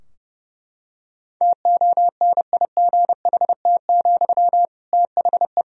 Morse code, heard as a keyed tone, about 32 wpm, 700 Hz. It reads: TONIGHT, THE
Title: Sentences from Top 1000 Words - Review Part 4 - 30wpm